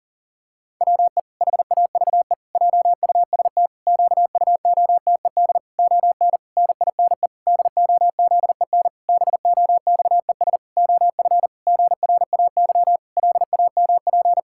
Morse code, heard as a keyed tone, about 40 wpm, 700 Hz. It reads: WE HAVE JUST QUOTED ON NINE DOZEN BOXES OF GRAY LAMP
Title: Pangram Challenge v2 - 40wpm